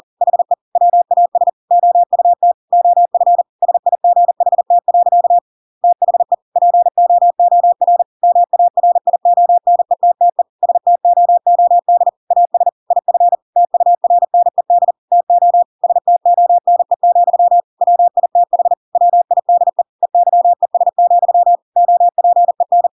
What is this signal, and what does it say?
Morse code, heard as a keyed tone, about 40 wpm, 700 Hz. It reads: HE WAS OUT OF SIGHT. THE POOR MARIONETTE STOOD AS IF TURNED TO STONE, WITH WIDE EYES, OPEN
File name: Pinocchio / Pinocchio ch5 40 40 ik1hge